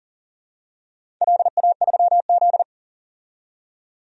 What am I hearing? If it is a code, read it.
Morse code, heard as a keyed tone, about 40 wpm, 700 Hz. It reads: LA3Z
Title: DXCC-100 Call Signs - 4 Characters - 40wpm